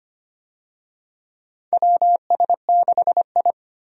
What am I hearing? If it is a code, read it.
Morse code, heard as a keyed tone, about 25 wpm, 700 Hz. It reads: WS6I
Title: USA Call Signs - 4 Characters - 25wpm